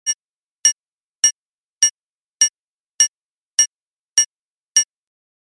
There is no code.